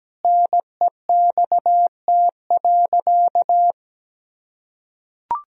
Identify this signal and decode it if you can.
Morse code, heard as a keyed tone, about 17 wpm, 700 Hz. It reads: NEXT.
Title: Sentences from Top 1000 Words - Review Part 2 - 17wpm